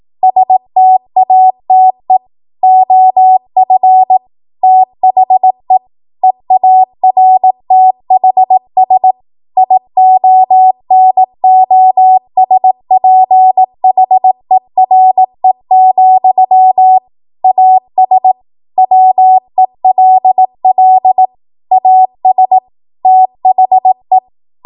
Morse code, heard as a keyed tone, about 18 wpm, 750 Hz. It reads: STATE OF THE EARTHS IONOSPHERE, AS WELL AS THE
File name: Morse / 18WPM